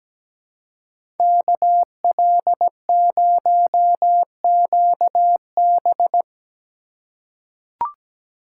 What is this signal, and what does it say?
Morse code, heard as a keyed tone, about 17 wpm, 700 Hz. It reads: KL0QB
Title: USA Call Signs - 2x Character Spacing - 17wpm